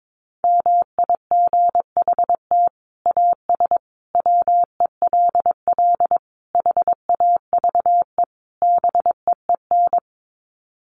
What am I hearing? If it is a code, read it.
Morse code, heard as a keyed tone, about 22 wpm, 700 Hz. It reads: MIGHT AS WELL HAVE BEEN
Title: Sets of 5 Words Encore2 22wpm